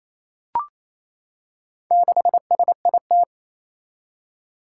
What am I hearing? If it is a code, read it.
Morse code, heard as a keyed tone, about 28 wpm, 700 Hz. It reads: 6SIT
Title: Four Challenging Letter-Number Combinations 28wpm